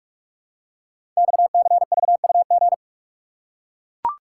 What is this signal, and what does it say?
Morse code, heard as a keyed tone, about 44 wpm, 700 Hz. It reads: XCVUG
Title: Code Groups - 5 Letter:Numbers - 45wpm